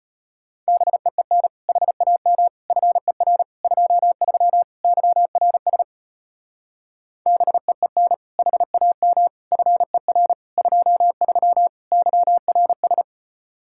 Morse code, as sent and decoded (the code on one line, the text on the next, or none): -... . -. / .... .- -- / ..-. .-. / ..--- ...-- / -.-- .-. ... / -... . -. / .... .- -- / ..-. .-. / ..--- ...-- / -.-- .-. ...
BEEN HAM FER 23 YRS BEEN HAM FER 23 YRS